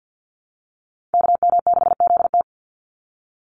Morse code, heard as a keyed tone, about 50 wpm, 700 Hz. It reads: KM6ZT
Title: Call Signs - 50wpm